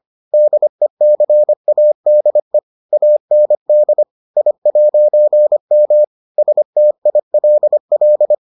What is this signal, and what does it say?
Morse code, heard as a keyed tone, about 25 wpm, 600 Hz. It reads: DECADE AND I'M STILL